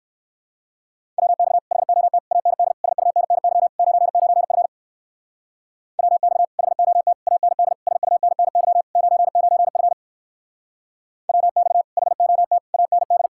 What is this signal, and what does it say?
Morse code, heard as a keyed tone, about 58 wpm, 700 Hz. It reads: WX HOT AND SUNNY 99F WX HOT AND SUNNY 99F WX HOT AND